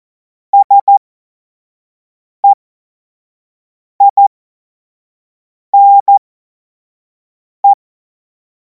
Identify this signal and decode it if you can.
Morse code, heard as a keyed tone, about 14 wpm, spaced as for 5 wpm, 800 Hz. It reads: SEINE
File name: Morsekurs / Witze / Witz 17. Februar 2026 25 F70